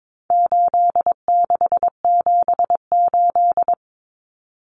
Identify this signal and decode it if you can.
Morse code, heard as a keyed tone, about 22 wpm, 700 Hz. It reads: 8678
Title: Numbers - 4-Digit 22wpm